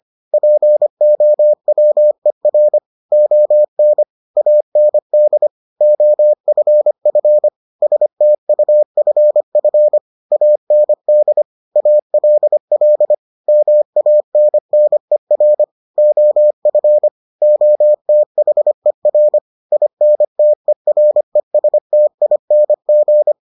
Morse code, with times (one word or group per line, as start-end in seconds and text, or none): 0.3-2.8 POWER
3.1-4.0 ON
4.4-5.5 AND
5.8-7.5 OFF
7.8-10.0 STUFF
10.3-11.4 AND
11.8-13.1 ALL
13.5-15.6 MANNER
16.0-17.1 OF
17.4-19.4 OTHER
19.7-23.3 INTERESTING